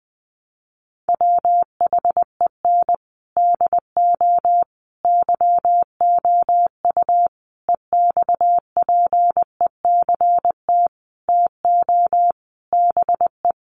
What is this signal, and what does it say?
Morse code, heard as a keyed tone, about 20 wpm, 700 Hz. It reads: WHEN DO YOU EXPECT TO BE
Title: Sentences from Top 1100 Words 20wpm